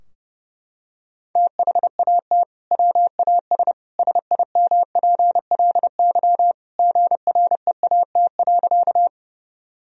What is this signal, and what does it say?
Morse code, heard as a keyed tone, about 32 wpm, 700 Hz. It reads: THAT WAS SIMPLY GREAT.
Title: Sentences from Top 1000 Words - Review Part 4 - 30wpm